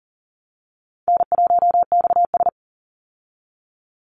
Morse code, heard as a keyed tone, about 40 wpm, 700 Hz. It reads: N1XS
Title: Easy Call Signs Repeated 3x - 40wpm